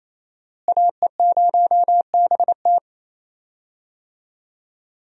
Morse code, heard as a keyed tone, about 28 wpm, 700 Hz. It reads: AE0BT
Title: USA Call Signs - 5 Characters - Repeated 2x - 28wpm